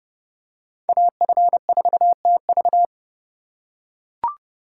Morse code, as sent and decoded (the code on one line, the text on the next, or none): .- ..-. ....- - ...-
AF4TV